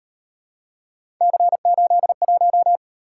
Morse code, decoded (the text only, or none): C81